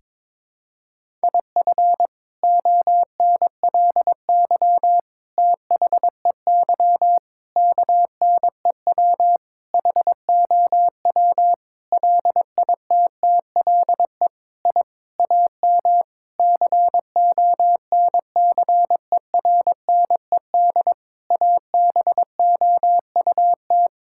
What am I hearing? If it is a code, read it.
Morse code, heard as a keyed tone, about 22 wpm, 700 Hz. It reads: IF ONLY THEY KNEW HOW LITTLE I AM CONCERNED ABOUT
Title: Sentences from Top 1000 Words - Review Part 6 - 22wpm